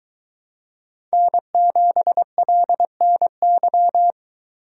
Morse code, heard as a keyed tone, about 22 wpm, 700 Hz. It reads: N7LNY